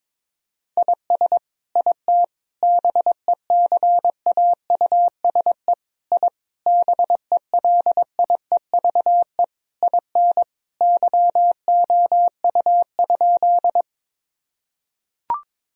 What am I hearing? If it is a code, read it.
Morse code, heard as a keyed tone, about 22 wpm, 700 Hz. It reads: IS IT BECAUSE I BELIEVE IN YOU?
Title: Sentences from Top 1000 Words - Review Part 3 - 22wpm